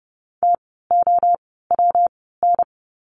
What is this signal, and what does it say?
Morse code, heard as a keyed tone, about 30 wpm, spaced as for 17 wpm, 700 Hz. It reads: TOWN
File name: course / 30:10wpm / 115 Lesson 19f - Letter w - Words 30wpm